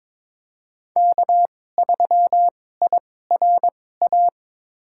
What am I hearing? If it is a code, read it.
Morse code, heard as a keyed tone, about 22 wpm, spaced as for 16 wpm, 700 Hz. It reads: K3IRA
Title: USA Call Signs - 5 Characters - 2x Character Spacing - 22wpm